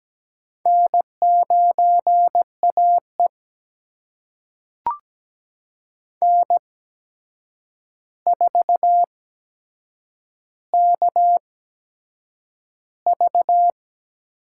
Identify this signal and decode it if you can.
Morse code, heard as a keyed tone, about 17 wpm, spaced as for 7 wpm, 700 Hz. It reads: N9AE N4KV
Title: USA Call Signs - 4 Characters - 8x Character Spacing - 17wpm